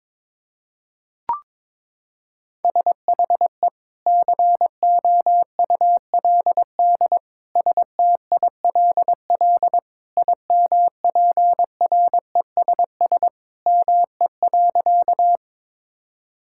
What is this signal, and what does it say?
Morse code, heard as a keyed tone, about 22 wpm, 700 Hz. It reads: SHE COULD STILL IMPRESS ME.
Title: Sentences from Top 1600 Words 22wpm